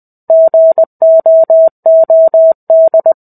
Morse code, heard as a keyed tone, about 20 wpm, 650 Hz. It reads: GOOD
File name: 05-Chapter-05 20 WPM